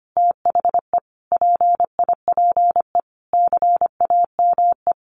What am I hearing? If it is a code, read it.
Morse code, heard as a keyed tone, about 25 wpm, 700 Hz. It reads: THE PIPE CAME
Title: Sentences from Top 1500 Words 25wpm